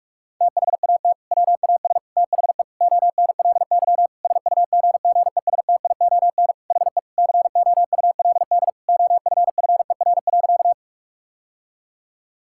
Morse code, heard as a keyed tone, about 41 wpm, 700 Hz. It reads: THAT WAS THE ONLY SUGGESTION HE COULD OFFER.
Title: Sentences from Top 1600 Words 45wpm